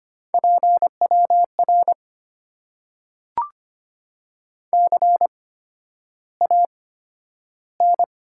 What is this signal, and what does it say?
Morse code, heard as a keyed tone, about 25 wpm, spaced as for 11 wpm, 700 Hz. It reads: PWR CAN
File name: Top 90 QSO Elements - 8x Character Spacing - 25wpm